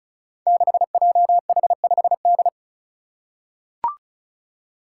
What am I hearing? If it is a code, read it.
Morse code, heard as a keyed tone, about 36 wpm, 700 Hz. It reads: 6JH5D
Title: Five Challenging Letter-Number Combinations 35wpm